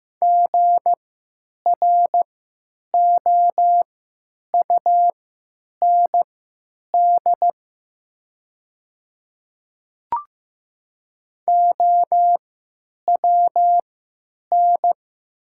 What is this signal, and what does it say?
Morse code, heard as a keyed tone, about 15 wpm, spaced as for 9 wpm, 700 Hz. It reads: GROUND OWN
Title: 3x Character Spacing - Top 500 Words - 15wpm